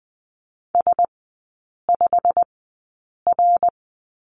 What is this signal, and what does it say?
Morse code, heard as a keyed tone, about 20 wpm, spaced as for 8 wpm, 690 Hz. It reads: S5R
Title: 191 Lesson 31e - Letter v - Three Characters 20wpm